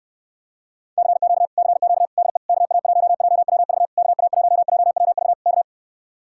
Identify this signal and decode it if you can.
Morse code, heard as a keyed tone, about 69 wpm, 700 Hz. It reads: CQ CQ DE KA1JWV KA1JWV K